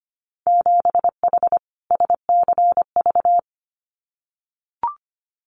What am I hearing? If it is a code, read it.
Morse code, heard as a keyed tone, about 25 wpm, 700 Hz. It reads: 7H SCV
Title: Winter Field Day Exchanges Encore 25wpm